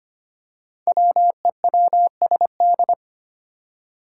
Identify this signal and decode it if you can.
Morse code, heard as a keyed tone, about 25 wpm, 700 Hz. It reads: WEWSD
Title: Five Challenging Letter-Number Combinations 25wpm